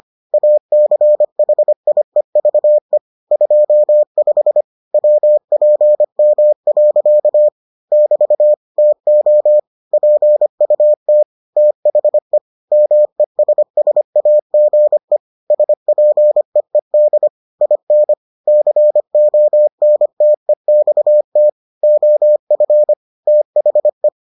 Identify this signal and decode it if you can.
Morse code, heard as a keyed tone, about 25 wpm, 600 Hz. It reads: ACHIEVE 25 WPM. <BT> TO PUT THE MESSAGE SPEED IN CONTEXT OF THE